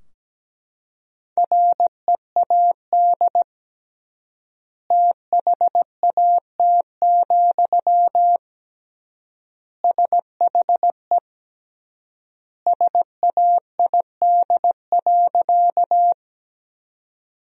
Morse code, heard as a keyed tone, about 17 wpm, 700 Hz. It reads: READ THAT, SHE SAID.
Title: 3x Word-Spacing - Sentences from Top 200 Words - 17wpm